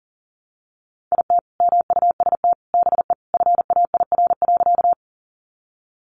Morse code, heard as a keyed tone, about 40 wpm, 700 Hz. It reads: IT MUST BE FAIR.
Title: Sentences from Top 1300 Words 40wpm